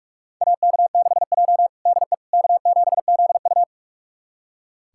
Morse code, heard as a keyed tone, about 43 wpm, 700 Hz. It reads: AK6J DE K7ZU
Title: USA Callsign DE Callsign - 4 Characters - 45wpm